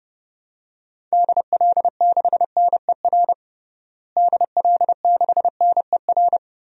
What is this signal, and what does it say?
Morse code, heard as a keyed tone, about 30 wpm, 700 Hz. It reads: DL6NER DL6NER
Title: DXCC-100 Call Signs - 6 Characters - Repeated 3x - 30wpm